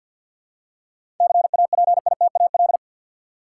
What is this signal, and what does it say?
Morse code, heard as a keyed tone, about 52 wpm, 700 Hz. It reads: CAPITAL
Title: Seven Letter Words - 50wpm